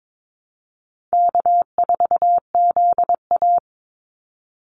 Morse code, heard as a keyed tone, about 22 wpm, 700 Hz. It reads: K4ZA
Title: Easy Call Signs Repeated 3x - 22wpm